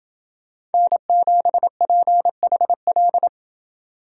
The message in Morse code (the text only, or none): N7PHL